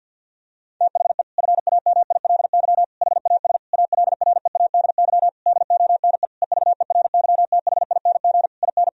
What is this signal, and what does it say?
Morse code, heard as a keyed tone, about 50 wpm, 700 Hz. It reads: THE FAMILY HAS ALREADY DONE EVERYTHING IN
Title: Sentences from Top 1000 Words - Review Part 2 - 50wpm